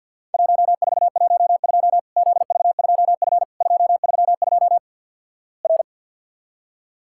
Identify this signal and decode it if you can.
Morse code, heard as a keyed tone, about 50 wpm, 700 Hz. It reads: 1412 ZV2F 232 R
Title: Worked All Europe DX Contest - QTC Exchange - 50wpm